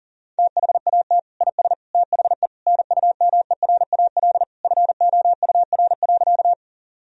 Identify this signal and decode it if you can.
Morse code, heard as a keyed tone, about 40 wpm, 700 Hz. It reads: THAT IS THE NUMERAL FOUR.
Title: Sentences from Top 1000 Words - Review Part 2 - 40wpm